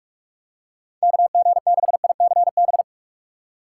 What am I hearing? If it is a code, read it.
Morse code, heard as a keyed tone, about 46 wpm, 700 Hz. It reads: KG6ICB
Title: USA Call Signs - 6 Characters - 6x Character Spacing - 45wpm